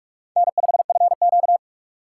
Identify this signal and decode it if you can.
Morse code, heard as a keyed tone, about 43 wpm, 700 Hz. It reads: N5FQ
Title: USA Call Signs - 4 Characters - 2x Character Spacing - 45wpm